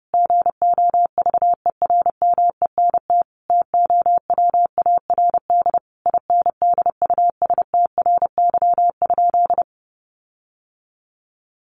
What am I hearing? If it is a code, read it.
Morse code, heard as a keyed tone, about 30 wpm, 700 Hz. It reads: GOVERMENT TOWARD INDUSTRY?
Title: Sentences from Top 1300 Words 30wpm